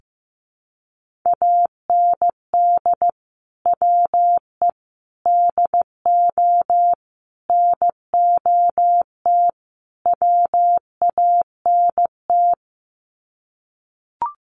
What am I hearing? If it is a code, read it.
Morse code, heard as a keyed tone, about 15 wpm, 700 Hz. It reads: AND WE DO NOT WANT
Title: Sets of 5 Words Encore2 15wpm